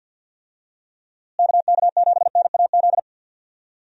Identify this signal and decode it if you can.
Morse code, heard as a keyed tone, about 52 wpm, 700 Hz. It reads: KK7NAZ
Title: USA Call Signs - 6 Characters - Repeated 3x - 50wpm